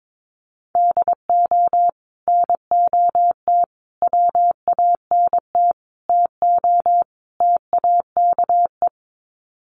Morse code, heard as a keyed tone, about 22 wpm, 700 Hz. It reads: DO NOT WANT TO TAKE
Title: Sets of 5 Words Encore2 22wpm